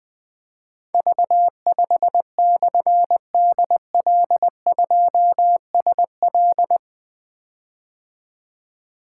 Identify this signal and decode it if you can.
Morse code, heard as a keyed tone, about 20 wpm, 700 Hz. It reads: V5/DL2SL
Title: Difficult Call Signs 20wpm